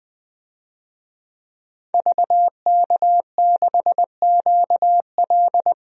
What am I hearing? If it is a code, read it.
Morse code, heard as a keyed tone, about 20 wpm, 700 Hz. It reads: VK6QL
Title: DXCC-100 Call Signs - 5 Characters - 20wpm